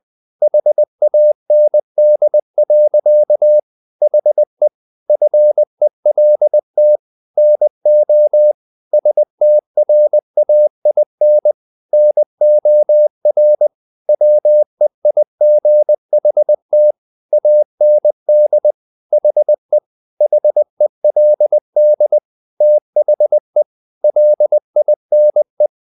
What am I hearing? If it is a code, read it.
Morse code, heard as a keyed tone, about 20 wpm, 600 Hz. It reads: HAND. HE FELT NO STRAIN NOR WEIGHT AND HE HELD THE LINE
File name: the old man and the sea / the old man and the sea part 1 20wpm